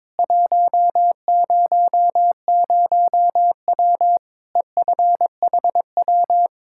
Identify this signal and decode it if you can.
Morse code, heard as a keyed tone, about 22 wpm, 700 Hz. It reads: 100W EFHW
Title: K1USN SST QSO Practice 22wpm